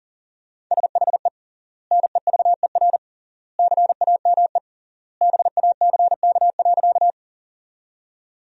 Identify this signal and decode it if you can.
Morse code, heard as a keyed tone, about 40 wpm, 700 Hz. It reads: SHE NEVER CAME BACK.
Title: 3x Word-Spacing - Sentences from Top 300 Words - 40wpm